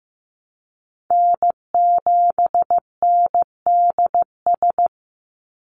Morse code, heard as a keyed tone, about 15 wpm, 700 Hz. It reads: N7NDS
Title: USA Call Signs 15wpm